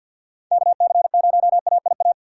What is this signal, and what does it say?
Morse code, heard as a keyed tone, about 47 wpm, 700 Hz. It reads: KC0AIA
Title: USA Call Signs - 3x Character Spacing - 50wpm